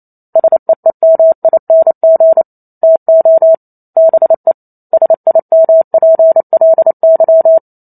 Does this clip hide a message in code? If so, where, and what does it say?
Yes: Morse code from 0.4 to 7.6 s, heard as a keyed tone, about 28 wpm, 650 Hz. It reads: SEEMING TO BE SIMPLY